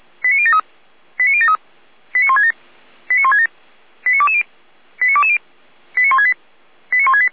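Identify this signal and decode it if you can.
Five-tone selective call (ZVEI), heard as quick runs of five tones, tones of about 80 ms. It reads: 89073 89073 89167 89167 89220 89220 89168 89168
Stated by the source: ZVEI five-tone selective call received on SDR.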